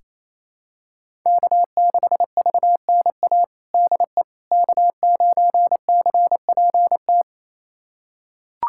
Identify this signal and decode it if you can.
Morse code, heard as a keyed tone, about 27 wpm, 700 Hz. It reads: K6VNA DE K9CPT
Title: USA Callsign DE Callsign - 5 Characters - 28wpm